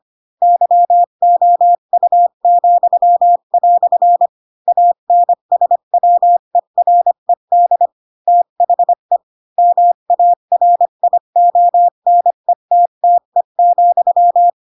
Morse code, heard as a keyed tone, about 25 wpm, 700 Hz. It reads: YOU," ANSWERED THE MARIONETTE,
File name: Pinocchio / Pinocchio ch8 20 25 ik1hge